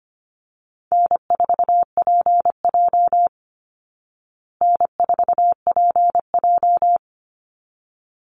Morse code, heard as a keyed tone, about 25 wpm, 700 Hz. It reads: N4PJ N4PJ
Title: Call Signs Repeated 3x - 25wpm